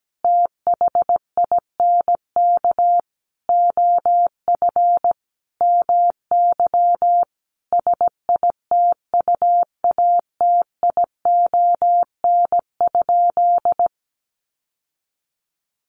Morse code, heard as a keyed tone, about 17 wpm, 700 Hz. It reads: THINK OF MY SITUATION?
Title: Sentences from Top 1200 Words 17wpm